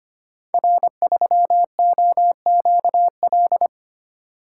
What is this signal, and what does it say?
Morse code, heard as a keyed tone, about 25 wpm, 700 Hz. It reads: R3OQL
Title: Code Groups - 5 Letter:Numbers - 25wpm